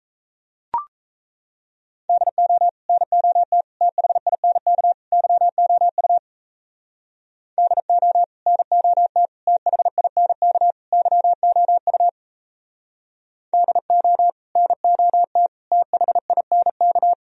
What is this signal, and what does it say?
Morse code, heard as a keyed tone, about 34 wpm, 700 Hz. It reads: DO NOT THINK YOU DO NOT THINK YOU DO NOT THINK
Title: Speed-Racing - Sets of 4 Words 28wpm